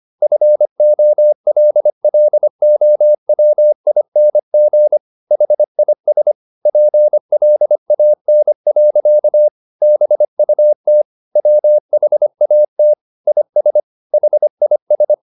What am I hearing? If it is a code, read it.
Morse code, heard as a keyed tone, about 25 wpm, 600 Hz. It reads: FOLLOWING HIS PLAN. BUT WHAT IS HIS
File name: the old man and the sea part 1 25wpm